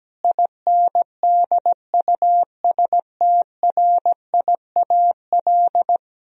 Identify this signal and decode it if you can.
Morse code, heard as a keyed tone, about 17 wpm, 700 Hz. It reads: INDUSTRIAL
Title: Speed-Racing - Sets of 5 Words 17wpm